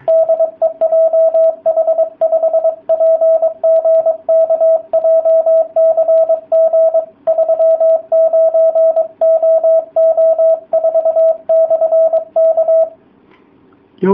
Morse code, heard as a keyed tone, about 23 wpm, 660 Hz, with radio noise. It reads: DE JH5PGK JCG39OO4/K